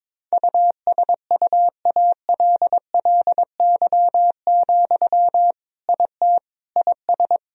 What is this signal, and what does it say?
Morse code, heard as a keyed tone, about 22 wpm, 700 Hz. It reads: USUALLY, IT IS
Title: Sentences from Top 1000 Words - Review Part 3 - 22wpm